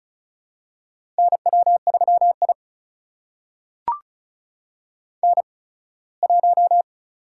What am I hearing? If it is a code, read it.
Morse code, heard as a keyed tone, about 36 wpm, spaced as for 21 wpm, 700 Hz. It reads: NW3I N1
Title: USA Call Signs - 4 Characters - 8x Character Spacing - 35wpm